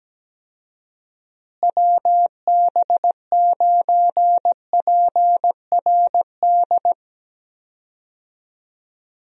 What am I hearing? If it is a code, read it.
Morse code, heard as a keyed tone, about 17 wpm, 700 Hz. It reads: WB9PRD
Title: USA Call Signs - 6 Characters - Repeated 2x - 17wpm